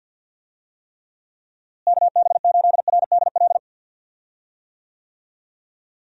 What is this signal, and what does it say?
Morse code, heard as a keyed tone, about 49 wpm, 700 Hz. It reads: KB8RDL
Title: USA Call Signs - 6 Characters - Repeated 2x - 50wpm